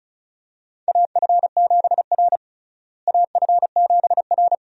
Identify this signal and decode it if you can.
Morse code, heard as a keyed tone, about 36 wpm, 700 Hz. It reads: AF7R AF7R
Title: USA Call Signs - 4 Characters - Repeated 2x - 35wpm